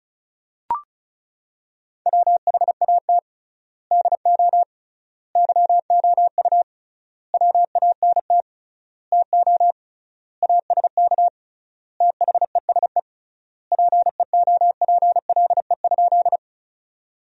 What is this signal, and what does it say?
Morse code, heard as a keyed tone, about 36 wpm, 700 Hz. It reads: WHAT DO YOU WANT TO ASK THESE PEOPLE?
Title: 3x Word-Spacing - Sentences from Top 200 Words - 35wpm